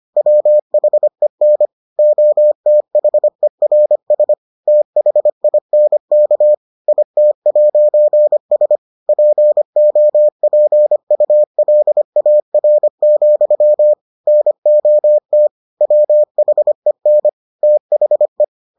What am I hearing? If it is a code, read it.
Morse code, heard as a keyed tone, about 25 wpm, 600 Hz. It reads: WHEN OTHERS THINK IT'S POPULAR, NOT WHEN THE